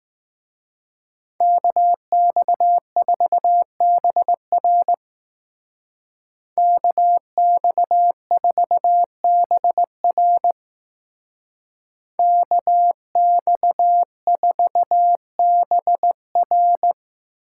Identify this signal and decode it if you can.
Morse code, heard as a keyed tone, about 19 wpm, 700 Hz. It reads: KX4BR KX4BR KX4BR
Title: Speed Racing - USA Call Signs 15wpm